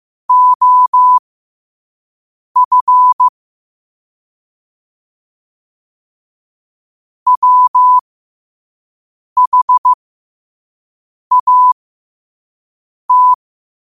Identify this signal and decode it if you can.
Morse code, heard as a keyed tone, about 15 wpm, spaced as for 5 wpm, 1000 Hz. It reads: OF WHAT
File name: qotdfiles / 5wpm